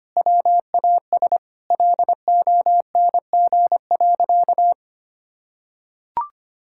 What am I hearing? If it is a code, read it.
Morse code, heard as a keyed tone, about 25 wpm, 700 Hz. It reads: WAS LONG.
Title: Sentences from Top 1600 Words 25wpm